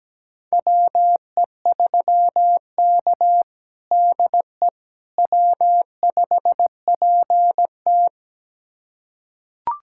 Morse code, heard as a keyed tone, about 17 wpm, 700 Hz. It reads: WE3K DE W5PT
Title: USA Callsign DE Callsign - 4 Characters - 17wpm